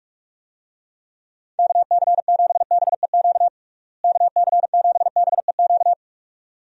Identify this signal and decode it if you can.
Morse code, heard as a keyed tone, about 46 wpm, 700 Hz. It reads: KC7BEQ KC7BEQ
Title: USA Call Signs - 6 Characters - Repeated 2x - 45wpm